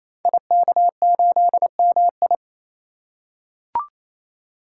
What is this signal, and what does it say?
Morse code, heard as a keyed tone, about 27 wpm, 700 Hz. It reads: IK8MI
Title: Code Groups - 5 Letter:Numbers - 28wpm